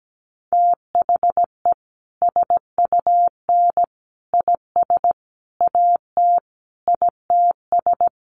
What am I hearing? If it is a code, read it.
Morse code, heard as a keyed tone, about 17 wpm, 700 Hz. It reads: THE SUN IS AT ITS
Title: Sentences from Top 1300 Words 17wpm